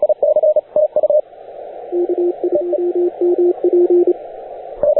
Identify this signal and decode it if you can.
Morse code, heard as a keyed tone, about 27 wpm, 350 Hz, with radio noise. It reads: KIWMP